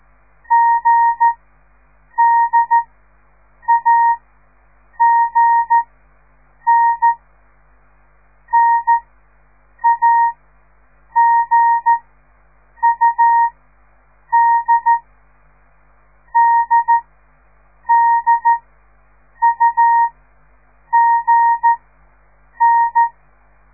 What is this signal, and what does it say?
Morse code, heard as a keyed tone, about 13 wpm, spaced as for 7 wpm, 930 Hz. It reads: GDAGN NAGUD DDUGN
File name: radiomorningstar / morse / les 4